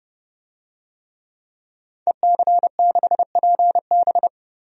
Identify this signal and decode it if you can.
Morse code, heard as a keyed tone, about 32 wpm, 700 Hz. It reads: EC6PB
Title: DXCC-100 Call Signs - 5 Characters - 30wpm